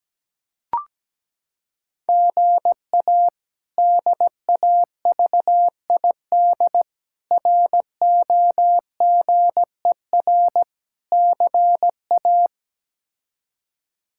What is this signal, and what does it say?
Morse code, heard as a keyed tone, about 17 wpm, 700 Hz. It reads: GA DAVID ROGER CA
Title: K1USN SST QSO Practice - 17wpm